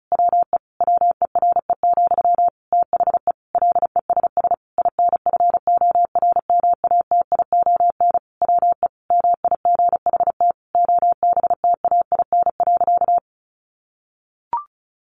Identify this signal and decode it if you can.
Morse code, heard as a keyed tone, about 34 wpm, 700 Hz. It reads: WE WERE, THE LESS INFORMATION WE MIGHT OBTAIN.
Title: Sentences from Top 1300 Words 35wpm